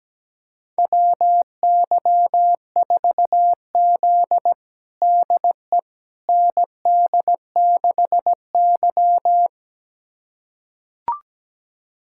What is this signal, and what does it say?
Morse code, heard as a keyed tone, about 17 wpm, 700 Hz. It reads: WY4Z DE ND6Y
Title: USA Callsign DE Callsign - 4 Characters - 17wpm